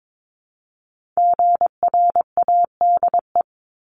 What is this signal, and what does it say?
Morse code, heard as a keyed tone, about 22 wpm, 700 Hz. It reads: GRADE
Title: Multi-letter Phonemes 22wpm